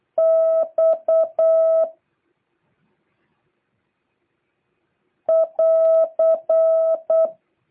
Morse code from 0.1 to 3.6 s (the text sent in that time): X